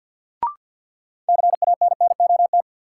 Morse code, heard as a keyed tone, about 53 wpm, 700 Hz. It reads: CANNOT
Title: Sentences from Top 1000 Words - Review Part 6 - New Word Drill - 50wpm